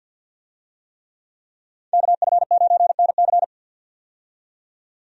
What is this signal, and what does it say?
Morse code, heard as a keyed tone, about 52 wpm, 700 Hz. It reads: KF9NC
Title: USA Call Signs - 5 Characters - 50wpm